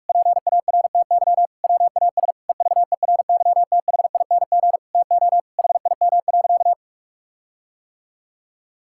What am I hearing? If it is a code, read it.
Morse code, heard as a keyed tone, about 46 wpm, 700 Hz. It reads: PARTY WAS EVERYTHING TO HIM.
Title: Sentences from Top 1600 Words 45wpm